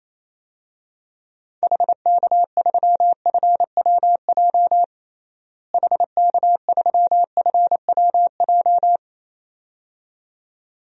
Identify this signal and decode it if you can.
Morse code, heard as a keyed tone, about 28 wpm, 700 Hz. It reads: HK3FWJ HK3FWJ
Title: DXCC-100 Call Signs - 6 Characters - Repeated 2x - 28wpm